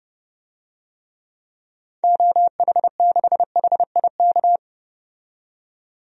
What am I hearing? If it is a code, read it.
Morse code, heard as a keyed tone, about 30 wpm, 700 Hz. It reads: OH6HIK
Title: DXCC-100 Call Signs - 6 Characters - 30wpm